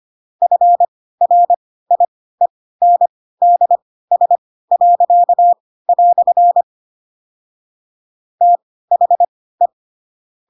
Morse code, heard as a keyed tone, about 25 wpm, spaced as for 16 wpm, 700 Hz. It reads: FRIENDS." THE
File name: Pinocchio / Pinocchio ch2 10 25 ik1hge